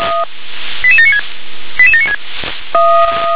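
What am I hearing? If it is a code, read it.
Five-tone selective call (ZVEI), heard as quick runs of five tones, tones of about 70 ms. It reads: 88796 88796
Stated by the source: ZVEI five-tone selective call received by SDR.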